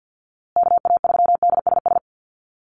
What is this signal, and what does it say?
Morse code, heard as a keyed tone, about 48 wpm, 700 Hz. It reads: KA3DSS
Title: USA Call Signs Repeated 3x 50wpm